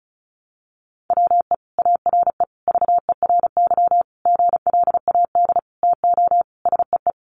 Morse code, heard as a keyed tone, about 34 wpm, 700 Hz. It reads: WE ARE VERY GLAD TO SEE
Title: Sentences from Top 800 Words 35wpm